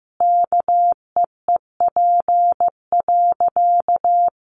Morse code, heard as a keyed tone, about 15 wpm, 700 Hz. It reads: KEEP.
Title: Sentences from Top 1100 Words 15wpm